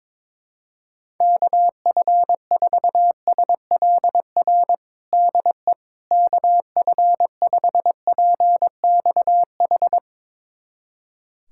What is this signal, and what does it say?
Morse code, heard as a keyed tone, about 22 wpm, 700 Hz. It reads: KF4SLR DE KF5PXH
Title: USA Callsign DE Callsign - 6 Characters - 22wpm